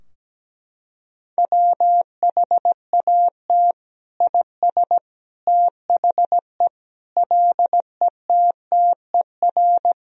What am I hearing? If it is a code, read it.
Morse code, heard as a keyed tone, about 17 wpm, 700 Hz. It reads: WHAT IS THE LETTER
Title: Sentences from Top 1000 Words - Review Part 2 - 17wpm